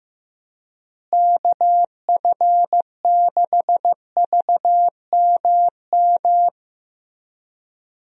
Morse code, heard as a keyed tone, about 15 wpm, 700 Hz. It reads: KF6VMM
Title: USA Call Signs - 6 Characters - Repeated 3x - 15wpm